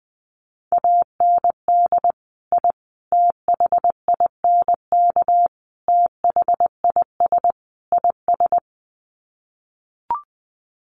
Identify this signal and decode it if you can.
Morse code, heard as a keyed tone, about 20 wpm, 700 Hz. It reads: AND I THINK THIS IS
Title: Sets of 5 Words Encore 20wpm